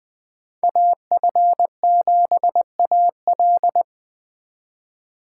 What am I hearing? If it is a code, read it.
Morse code, heard as a keyed tone, about 20 wpm, 700 Hz. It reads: AF7AL